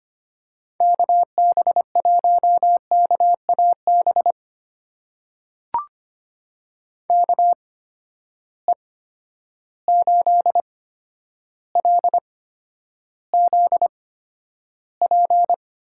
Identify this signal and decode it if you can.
Morse code, heard as a keyed tone, about 25 wpm, spaced as for 11 wpm, 700 Hz. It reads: KB1KAB KE8LZP
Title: USA Call Signs - 6 Characters - 8x Character Spacing - 25wpm